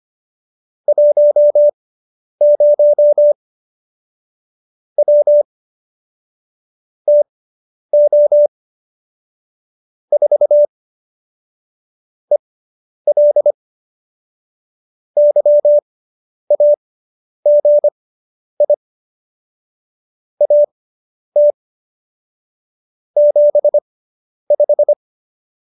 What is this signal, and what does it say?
Morse code, heard as a keyed tone, about 25 wpm, spaced as for 10 wpm, 590 Hz. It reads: 10 W TO 4 EL YAGI AT 75